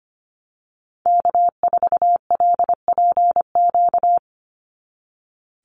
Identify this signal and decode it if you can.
Morse code, heard as a keyed tone, about 25 wpm, 700 Hz. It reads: K4LPQ
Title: NA SOTA Activator Callsigns 25wpm